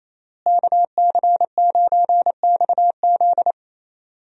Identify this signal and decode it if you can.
Morse code, heard as a keyed tone, about 28 wpm, 700 Hz. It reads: KC9XZ
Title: USA Call Signs - 5 Characters - Repeated 2x - 28wpm